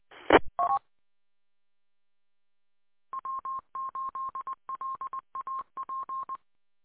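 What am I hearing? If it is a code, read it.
Morse code, heard as a keyed tone, about 20 wpm, 1060 Hz. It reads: ET W8LAP
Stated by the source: Repeater W8LAP 146.620 audio archive.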